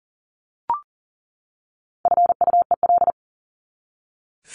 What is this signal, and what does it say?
Morse code, heard as a keyed tone, about 37 wpm, 700 Hz. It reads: FUEL
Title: Top 1751-2000 Words 40wpm